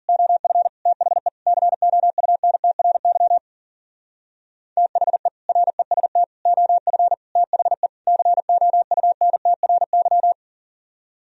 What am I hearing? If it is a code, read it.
Morse code, heard as a keyed tone, about 49 wpm, 700 Hz. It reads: OF THE COUNTRY THE REST OF THE COUNTRY
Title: Speed-Racing - Sets of 5 Words 40wpm